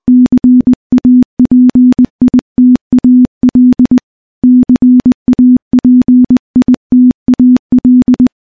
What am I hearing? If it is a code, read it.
Morse code, heard as a keyed tone, about 20 wpm, 250 Hz. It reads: CAPITAL CAPITAL
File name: morse / morse-training-20WPM